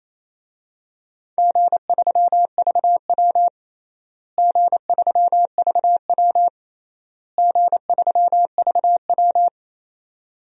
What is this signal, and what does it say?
Morse code, heard as a keyed tone, about 28 wpm, 700 Hz. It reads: G3VW G3VW G3VW
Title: DXCC-100 Call Signs - 4 Characters - Repeated 3x - 28wpm